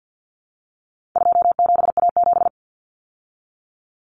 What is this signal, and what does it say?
Morse code, heard as a keyed tone, about 50 wpm, 700 Hz. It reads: 27A7
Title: Common Cut Numbers - 4-Digit 50wpm